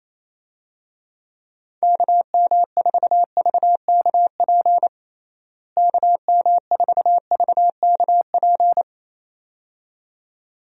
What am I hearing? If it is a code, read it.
Morse code, heard as a keyed tone, about 28 wpm, 700 Hz. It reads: KM4VKP KM4VKP
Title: USA Call Signs - 6 Characters - Repeated 2x - 28wpm